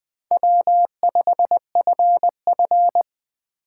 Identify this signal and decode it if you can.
Morse code, heard as a keyed tone, about 23 wpm, 700 Hz. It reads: W5FF
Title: Speed Racing - USA 1x2 Call Signs 20wpm